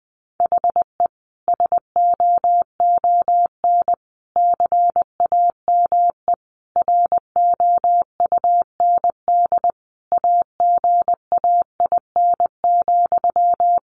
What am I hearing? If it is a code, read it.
Morse code, heard as a keyed tone, about 20 wpm, 700 Hz. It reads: HE SOON CAME ROUND AGAIN,
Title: Sentences from Top 600 Words 20wpm